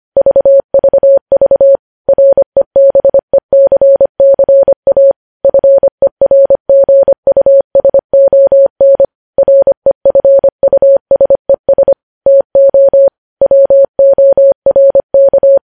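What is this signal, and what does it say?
Morse code, heard as a keyed tone, about 25 wpm, 550 Hz. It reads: VVV REBECCA FERGUSON REFUSES TO WORK
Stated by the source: News headlines sent in Morse at 25 wpm.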